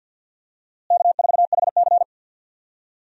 Morse code, heard as a keyed tone, about 53 wpm, 700 Hz. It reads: K4HC
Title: USA Call Signs - 4 Characters - Repeated 2x - 50wpm